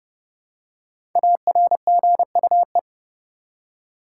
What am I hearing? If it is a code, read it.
Morse code, heard as a keyed tone, about 30 wpm, 700 Hz. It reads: ARGUE
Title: Five Letter Words - 30wpm